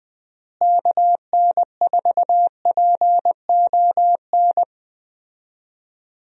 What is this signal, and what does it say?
Morse code, heard as a keyed tone, about 20 wpm, 700 Hz. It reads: KN4PON
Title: USA Call Signs - 6 Characters - Repeated 3x - 20wpm